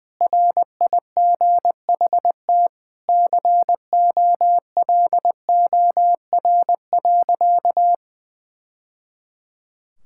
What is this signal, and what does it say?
Morse code, heard as a keyed tone, about 21 wpm, 700 Hz. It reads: RIGHT COLOR.
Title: Sentences from Top 1000 Words - Review Part 6 - 20wpm